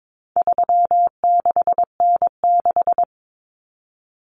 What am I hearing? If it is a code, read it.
Morse code, heard as a keyed tone, about 22 wpm, 700 Hz. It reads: 36N6
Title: Common Cut Numbers - 4-Digit 22wpm